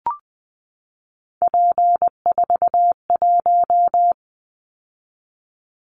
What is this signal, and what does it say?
Morse code, heard as a keyed tone, about 20 wpm, 700 Hz. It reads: P41
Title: Callsign Prefixes for Top 100 DXCC - 20wpm